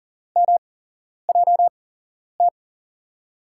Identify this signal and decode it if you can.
Morse code, heard as a keyed tone, about 40 wpm, spaced as for 11 wpm, 700 Hz. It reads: MJT
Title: USA Call Signs - 6 Characters - 8x Character Spacing - 40wpm